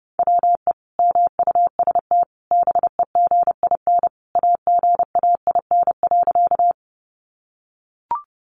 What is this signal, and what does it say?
Morse code, heard as a keyed tone, about 30 wpm, 700 Hz. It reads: WE MUST BEGIN AGAIN.
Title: Sentences from Top 300 Words 30wpm